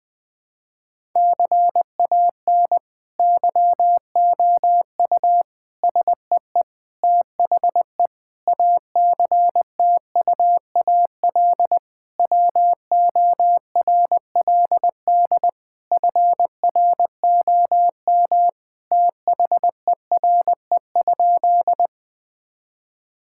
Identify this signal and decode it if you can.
Morse code, heard as a keyed tone, about 20 wpm, 700 Hz. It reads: CAN YOU SEE THE ACTUAL WORLD FROM THERE?
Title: Sentences from Top 1000 Words - Review Part 5 - 20wpm